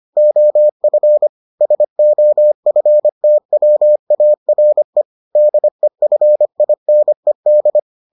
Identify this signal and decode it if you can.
Morse code, heard as a keyed tone, about 25 wpm, 600 Hz. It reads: OF SOFTWARE DEFINED